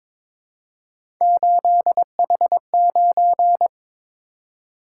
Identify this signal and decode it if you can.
Morse code, heard as a keyed tone, about 22 wpm, 700 Hz. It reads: 8H9
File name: practice / 22wpm / Three Letter-Number Combinations - 22wpm